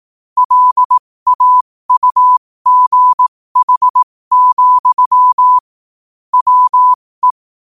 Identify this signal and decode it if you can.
Morse code, heard as a keyed tone, about 18 wpm, spaced as for 15 wpm, 1000 Hz. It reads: LAUGH, WE
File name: qotdfiles / 15wpm